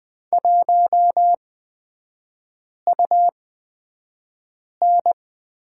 Morse code, heard as a keyed tone, about 20 wpm, spaced as for 5 wpm, 700 Hz. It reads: 1UN
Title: 250 Lesson 40e - BK - Three Characters 20wpm